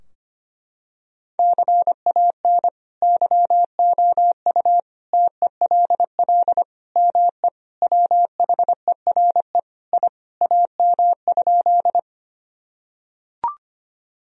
Morse code, heard as a keyed tone, about 25 wpm, 700 Hz. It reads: CAN YOU TELL ME WHERE I AM?
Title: Sentences from Top 1000 Words - Review Part 2 - 25wpm